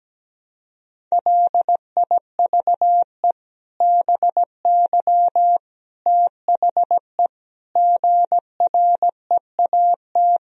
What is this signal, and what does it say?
Morse code, heard as a keyed tone, about 17 wpm, 700 Hz. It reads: LIVE BY THE GREAT
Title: Sentences from Top 1000 Words - Review Part 3 - 17wpm